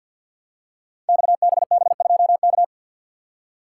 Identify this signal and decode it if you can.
Morse code, heard as a keyed tone, about 52 wpm, 700 Hz. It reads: XBB2K
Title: Code Groups - 5 Letter:Numbers - 50wpm